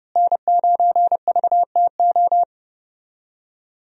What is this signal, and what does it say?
Morse code, heard as a keyed tone, about 29 wpm, 700 Hz. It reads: N9VTO
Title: USA Call Signs - 5 Characters - Repeated 3x - 30wpm